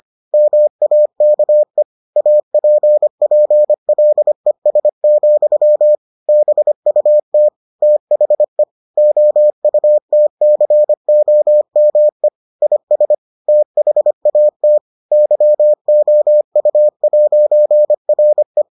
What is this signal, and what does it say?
Morse code, heard as a keyed tone, about 25 wpm, 600 Hz. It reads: MAKE APPLES, BUT THE OUTCOME IS THAT YOU'RE